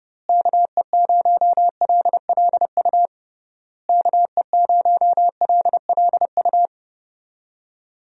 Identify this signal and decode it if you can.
Morse code, heard as a keyed tone, about 30 wpm, 700 Hz. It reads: KE0LLU KE0LLU
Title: USA Call Signs - 6 Characters - Repeated 3x - 30wpm